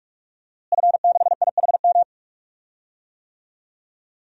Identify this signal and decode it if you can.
Morse code, heard as a keyed tone, about 42 wpm, 700 Hz. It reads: F6IHM
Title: DXCC-100 Call Signs - 5 Characters - Repeated 3x - 45wpm